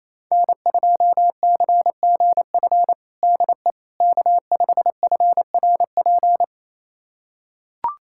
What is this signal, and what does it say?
Morse code, heard as a keyed tone, about 27 wpm, 700 Hz. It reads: N2CGF DE K5FRP
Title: USA Callsign DE Callsign - 5 Characters - 28wpm